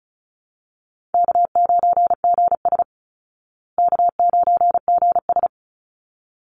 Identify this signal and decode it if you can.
Morse code, heard as a keyed tone, about 34 wpm, 700 Hz. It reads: K9GS K9GS
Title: Call Signs Repeated 3x - 35wpm